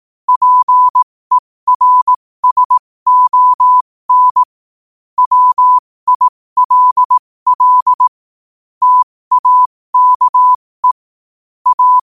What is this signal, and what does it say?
Morse code, heard as a keyed tone, about 18 wpm, spaced as for 15 wpm, 1000 Hz. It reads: PERSON WILL TAKE A
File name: qotdfiles / 15wpm